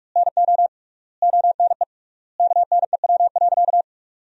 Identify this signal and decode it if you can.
Morse code, heard as a keyed tone, about 42 wpm, 700 Hz. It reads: NO ONE KNEW.
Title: 3x Word-Spacing - Sentences from Top 400 Words - 45wpm